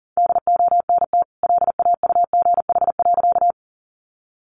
Morse code, heard as a keyed tone, about 40 wpm, 700 Hz. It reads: DONT LAUGH.
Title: Sentences from Top 500 Words 40wpm